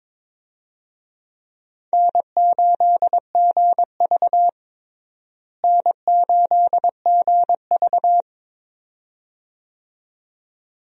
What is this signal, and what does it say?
Morse code, heard as a keyed tone, about 22 wpm, 700 Hz. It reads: N8GV N8GV
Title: USA Call Signs - 4 Characters - Repeated 2x - 22wpm